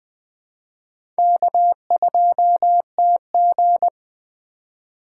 Morse code, heard as a keyed tone, about 21 wpm, 700 Hz. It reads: K2TG
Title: USA Call Signs - 4x Character Spacing - 20wpm